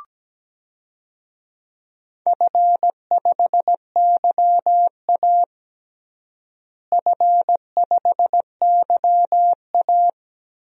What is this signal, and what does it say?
Morse code, heard as a keyed tone, about 17 wpm, 700 Hz. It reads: F5YA F5YA
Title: DXCC-100 Call Signs - 4 Characters - Repeated 3x - 17wpm